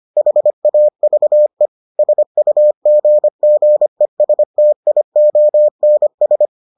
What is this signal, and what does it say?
Morse code, heard as a keyed tone, about 25 wpm, 600 Hz. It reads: HAVE SUGGESTIONS